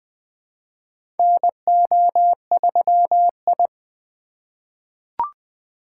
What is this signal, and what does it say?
Morse code, heard as a keyed tone, about 20 wpm, 700 Hz. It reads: NO3I
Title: USA Call Signs - 4 Characters - 2x Character Spacing - 20wpm